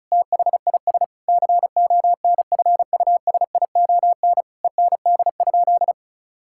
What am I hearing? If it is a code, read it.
Morse code, heard as a keyed tone, about 36 wpm, 700 Hz. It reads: THIS CONFUSION END?
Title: Sentences from Top 1600 Words 35wpm